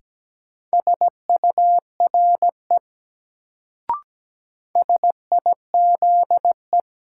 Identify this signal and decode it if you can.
Morse code, heard as a keyed tone, about 17 wpm, 700 Hz. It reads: SURE SIZE
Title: Sets of 3 Words - New Word Drill - 17wpm